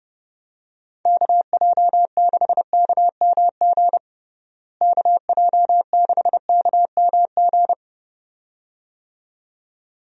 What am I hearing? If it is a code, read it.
Morse code, heard as a keyed tone, about 32 wpm, 700 Hz. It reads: KJ6KMG KJ6KMG
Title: USA Call Signs - 6 Characters - Repeated 2x - 30wpm